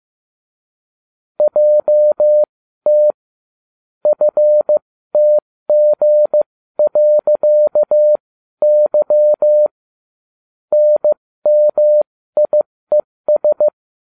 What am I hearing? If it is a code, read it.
Morse code, heard as a keyed tone, about 15 wpm, spaced as for 13 wpm, 600 Hz. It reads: JT FTG.Y NMIES